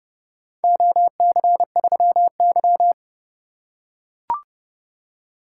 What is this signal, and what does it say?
Morse code, heard as a keyed tone, about 28 wpm, 700 Hz. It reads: OC3Y